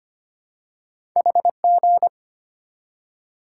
Morse code, heard as a keyed tone, about 25 wpm, 700 Hz. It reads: HG
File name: Two Letter-Number Combinations - 25wpm